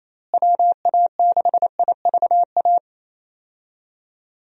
Morse code, heard as a keyed tone, about 28 wpm, 700 Hz. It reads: WA6IVA
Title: USA Call Signs - 6 Characters - Repeated 3x - 28wpm